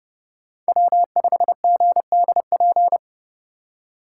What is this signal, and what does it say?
Morse code, heard as a keyed tone, about 30 wpm, 700 Hz. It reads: W5GDP